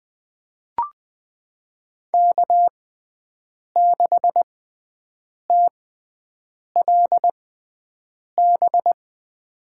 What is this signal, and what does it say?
Morse code, heard as a keyed tone, about 20 wpm, spaced as for 7 wpm, 700 Hz. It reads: K6TLB
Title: USA Call Signs - 5 Characters - 6x Character Spacing - 20wpm